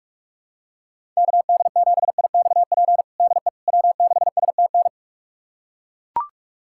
Morse code, heard as a keyed tone, about 42 wpm, 700 Hz. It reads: KD7IXP DE W6STN
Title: USA Callsign DE Callsign 45wpm